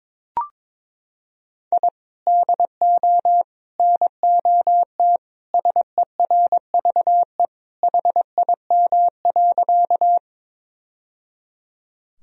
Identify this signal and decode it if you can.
Morse code, heard as a keyed tone, about 22 wpm, 700 Hz. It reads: I DO NOT SERVE HIM.
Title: Sentences from Top 1000 Words - Review Part 2 - 22wpm